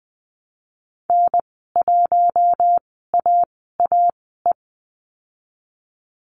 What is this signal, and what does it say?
Morse code, heard as a keyed tone, about 20 wpm, spaced as for 15 wpm, 700 Hz. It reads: N1AAE
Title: 048 Lesson 9g - Number 4 - Call Signs 20wpm